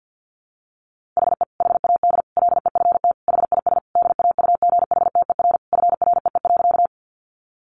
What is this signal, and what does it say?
Morse code, heard as a keyed tone, about 50 wpm, 700 Hz. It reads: HE HAD LEFT HIS DAUGHTER FREE.
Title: Sentences from Top 1500 Words 50wpm